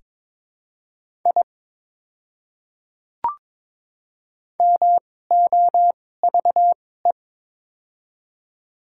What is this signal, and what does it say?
Morse code, heard as a keyed tone, about 21 wpm, spaced as for 16 wpm, 700 Hz. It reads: I MOVE
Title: 2x Character Spacing - Top 300 Words - 22wpm